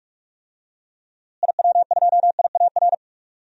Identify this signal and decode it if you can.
Morse code, heard as a keyed tone, about 46 wpm, 700 Hz. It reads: IW2IAR
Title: DXCC-100 Call Signs - 6 Characters - Repeated 2x - 45wpm